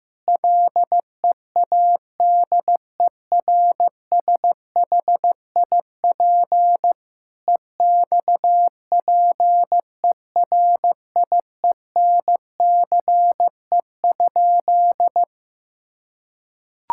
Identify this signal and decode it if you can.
Morse code, heard as a keyed tone, about 15 wpm, 700 Hz. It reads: LEADERSHIP EXPERIENCE?
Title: Sentences from Top 1600 Words 15wpm